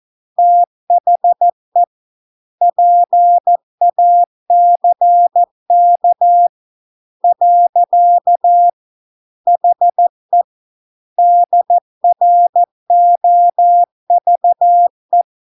Morse code, heard as a keyed tone, about 14 wpm, 700 Hz. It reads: THE PACK . HE DROVE